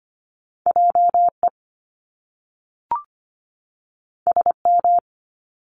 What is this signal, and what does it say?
Morse code, heard as a keyed tone, about 25 wpm, 700 Hz. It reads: JE SM
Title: Top Two Letter Combinations in Words 25wpm